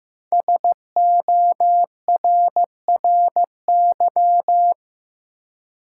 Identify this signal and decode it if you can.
Morse code, heard as a keyed tone, about 15 wpm, 700 Hz. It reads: SORRY
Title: Five Letter Words - 15wpm